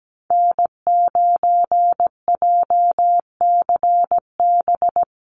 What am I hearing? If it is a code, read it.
Morse code, heard as a keyed tone, about 17 wpm, 700 Hz. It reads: N9JCB
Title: USA Call Signs Repeated 2x 17wpm